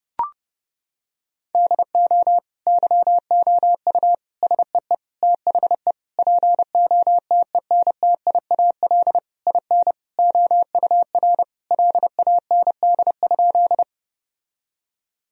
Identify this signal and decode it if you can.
Morse code, heard as a keyed tone, about 30 wpm, 700 Hz. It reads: DO YOU SEE THE POTENTIAL IN OUR LAND?
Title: Sentences from Top 1000 Words - Review Part 4 - 30wpm